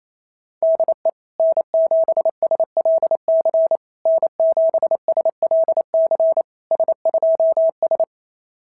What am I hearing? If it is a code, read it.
Morse code, heard as a keyed tone, about 25 wpm, 650 Hz. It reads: DE N7SLC N7SLC S2S
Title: SOTA QSO Practice 28wpm